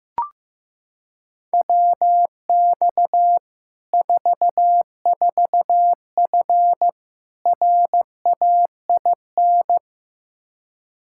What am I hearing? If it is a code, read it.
Morse code, heard as a keyed tone, about 15 wpm, 700 Hz. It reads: WX 44F RAIN
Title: Practice QSO - Exchange 2 - Info - 15wpm